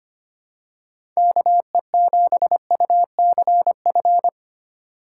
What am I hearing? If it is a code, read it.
Morse code, heard as a keyed tone, about 25 wpm, 700 Hz. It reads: KE7UCF
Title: USA Call Signs - 3x Character Spacing - 25wpm